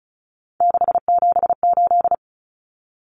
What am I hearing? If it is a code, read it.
Morse code, heard as a keyed tone, about 34 wpm, 700 Hz. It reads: 678
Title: Numbers - 3-Digit 35wpm